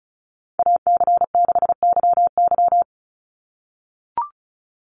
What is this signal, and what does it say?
Morse code, heard as a keyed tone, about 34 wpm, 700 Hz. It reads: AC6YY
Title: Call Signs - 35wpm